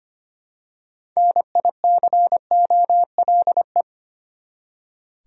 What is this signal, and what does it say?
Morse code, heard as a keyed tone, about 25 wpm, 700 Hz. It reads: NICOLE
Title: USA Common Names - 25wpm